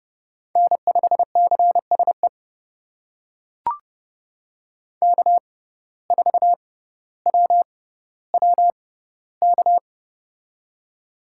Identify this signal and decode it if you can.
Morse code, heard as a keyed tone, about 30 wpm, spaced as for 15 wpm, 700 Hz. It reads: N5CSE K4WWK
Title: USA Call Signs - 5 Characters - 6x Character Spacing - 30wpm